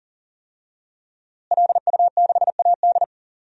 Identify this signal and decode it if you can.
Morse code, heard as a keyed tone, about 40 wpm, 700 Hz. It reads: LU6AD
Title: DXCC-100 Call Signs - 5 Characters - Repeated 3x - 40wpm